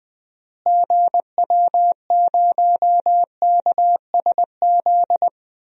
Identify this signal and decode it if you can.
Morse code, heard as a keyed tone, about 20 wpm, 700 Hz. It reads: GW0KSZ
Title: DXCC-100 Call Signs - 6 Characters - Repeated 3x - 20wpm